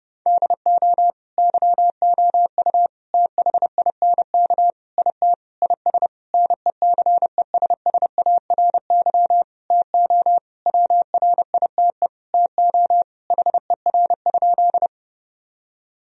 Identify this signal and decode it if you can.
Morse code, heard as a keyed tone, about 30 wpm, 700 Hz. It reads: DO YOU THINK IT IS NECESSARY TO WRITE TO HER?
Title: Sentences from Top 1000 Words - Review Part 1 - 30wpm